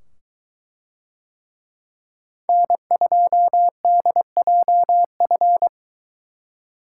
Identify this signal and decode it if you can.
Morse code, heard as a keyed tone, about 22 wpm, 700 Hz. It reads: N2DJF